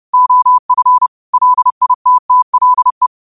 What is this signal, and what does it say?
Morse code, heard as a keyed tone, about 30 wpm, 1000 Hz. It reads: OF LITTLE
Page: New quotes every day in morse code at 30 Words per minute.